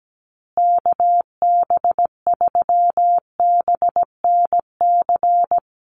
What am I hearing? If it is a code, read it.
Morse code, heard as a keyed tone, about 17 wpm, 700 Hz. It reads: KB3BNC
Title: USA Call Signs Repeated 3x 17wpm